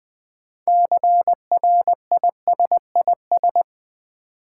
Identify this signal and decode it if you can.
Morse code, heard as a keyed tone, about 20 wpm, 700 Hz. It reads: CRISIS
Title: 6 Letter Words - First 3 Letters Given - 20wpm